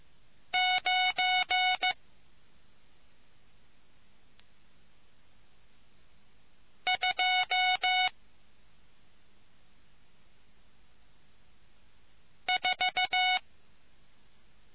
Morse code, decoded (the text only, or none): 9 2 4